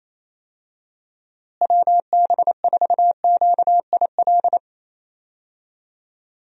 Morse code, heard as a keyed tone, about 28 wpm, 700 Hz. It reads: WB4QIL
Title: USA Call Signs - 6 Characters - Repeated 3x - 28wpm